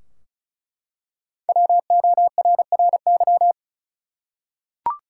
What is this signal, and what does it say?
Morse code, heard as a keyed tone, about 36 wpm, 700 Hz. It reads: WORRY
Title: Five Letter Words - 35wpm